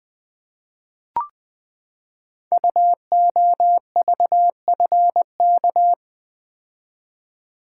Morse code, heard as a keyed tone, about 20 wpm, 700 Hz. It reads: UOVFK
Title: Code Groups - 5 Letters - 20wpm